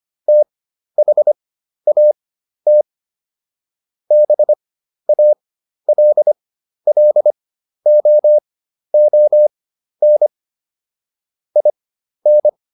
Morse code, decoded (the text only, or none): THAT BALLOON IN